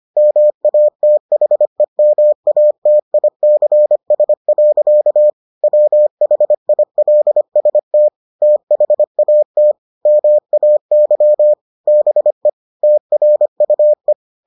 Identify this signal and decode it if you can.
Morse code, heard as a keyed tone, about 25 wpm, 600 Hz. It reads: MATHEMATICS. WHILST THAT MAY BE TRUE